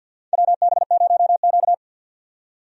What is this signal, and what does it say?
Morse code, heard as a keyed tone, about 53 wpm, 700 Hz. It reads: WB0Q
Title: USA Call Signs - 4 Characters - 50wpm